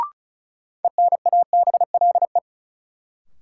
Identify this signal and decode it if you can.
Morse code, heard as a keyed tone, about 34 wpm, 700 Hz. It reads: ENABLE
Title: Sentences from Top 1500 Words - New Word Drill Alt - 35wpm